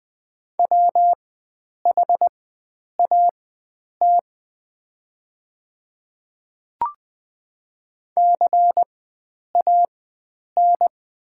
Morse code, heard as a keyed tone, about 20 wpm, spaced as for 9 wpm, 700 Hz. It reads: WHAT CAN
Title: 4x Character Spacing - Top 100 Words - 20wpm